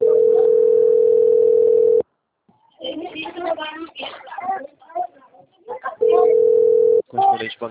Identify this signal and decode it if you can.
Ringback tone (North America), 440 + 480 Hz, with speech.
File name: out- - - - - .72328